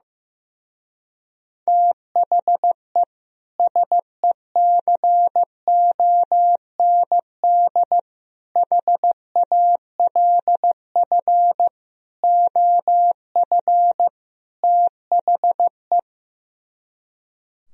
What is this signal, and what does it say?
Morse code, heard as a keyed tone, about 15 wpm, 700 Hz. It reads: THE SECOND HALF OF THE
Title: Speed-Racing - Sets of 5 Words 15wpm